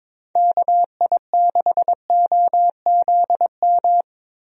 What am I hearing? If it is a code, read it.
Morse code, heard as a keyed tone, about 22 wpm, 700 Hz. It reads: KI6OZM
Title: USA Call Signs - 6 Characters - Repeated 3x - 22wpm